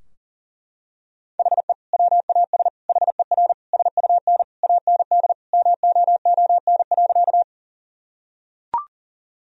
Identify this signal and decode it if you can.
Morse code, heard as a keyed tone, about 40 wpm, 700 Hz. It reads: HE WAS HER SUN AND MOON.
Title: Sentences from Top 1000 Words - Review Part 3 - 40wpm